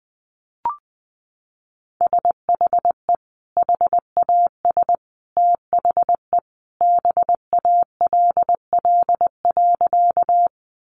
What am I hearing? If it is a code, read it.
Morse code, heard as a keyed tone, about 20 wpm, 700 Hz. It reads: SHE HAS THE BALL.
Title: Sentences from Top 900 Words 20wpm